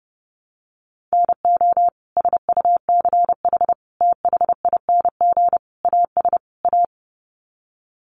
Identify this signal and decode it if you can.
Morse code, heard as a keyed tone, about 30 wpm, 700 Hz. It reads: NO SUCH THING AS A
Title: Sets of 5 Words Encore 30wpm